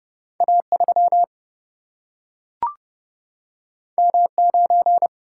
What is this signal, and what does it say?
Morse code, heard as a keyed tone, about 29 wpm, 700 Hz. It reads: A3 M9
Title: Single-Letter Single-Number 30wpm